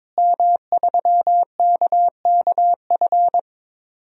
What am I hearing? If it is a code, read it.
Morse code, heard as a keyed tone, about 22 wpm, 700 Hz. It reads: M3KKF
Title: DXCC-100 Call Signs - 5 Characters - Repeated 3x - 22wpm